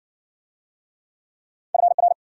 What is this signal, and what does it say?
Morse code, heard as a keyed tone, about 53 wpm, 700 Hz. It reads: FR